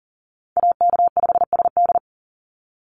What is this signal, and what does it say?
Morse code, heard as a keyed tone, about 40 wpm, 700 Hz. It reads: AK5SD
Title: NA SOTA Chaser Callsigns 40wpm